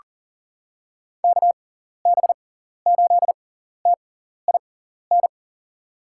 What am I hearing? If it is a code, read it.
Morse code, heard as a keyed tone, about 39 wpm, spaced as for 14 wpm, 700 Hz. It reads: KB8TIN
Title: USA Call Signs - 6x Character Spacing - 40wpm